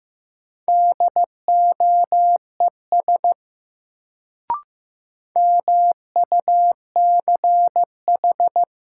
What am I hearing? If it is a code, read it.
Morse code, heard as a keyed tone, about 15 wpm, 700 Hz. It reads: DOES MUCH
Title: Sentences from Top 200 Words - New Word Drill - 15wpm